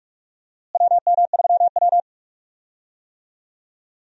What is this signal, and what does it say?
Morse code, heard as a keyed tone, about 42 wpm, 700 Hz. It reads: WM3W
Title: USA Call Signs - 4 Characters - Repeated 2x - 45wpm